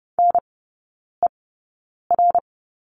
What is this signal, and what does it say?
Morse code, heard as a keyed tone, about 30 wpm, spaced as for 9 wpm, 700 Hz. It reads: NER
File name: 081 Lesson 14e - Number 2 - Three Characters 30wpm